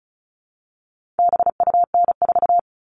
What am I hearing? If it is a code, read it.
Morse code, heard as a keyed tone, about 34 wpm, 700 Hz. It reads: BUN4
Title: All Cut Numbers - 4-Digit 35wpm